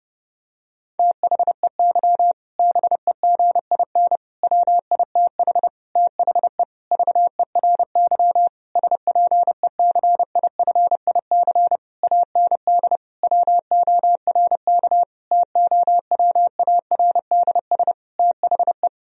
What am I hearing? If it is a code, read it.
Morse code, heard as a keyed tone, about 30 wpm, 700 Hz. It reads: THEY BEGIN WITH THE VERY SPECIFIC AND WORK TOWARDS THE
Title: Sentences from Top 1000 Words - Review Part 3 - 30wpm